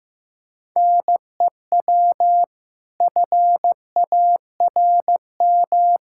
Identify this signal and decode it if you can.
Morse code, heard as a keyed tone, about 15 wpm, 700 Hz. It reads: NEW FARM
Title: Pangram Challenge v2 - 15wpm